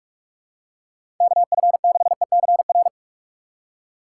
Morse code, heard as a keyed tone, about 46 wpm, 700 Hz. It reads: KF6ECR
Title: USA Call Signs - 6 Characters - 4x Character Spacing - 45wpm